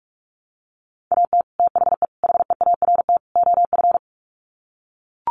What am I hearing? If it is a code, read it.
Morse code, heard as a keyed tone, about 44 wpm, 700 Hz. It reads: AT THE HEART OF
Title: Sets of 4 Words 45wpm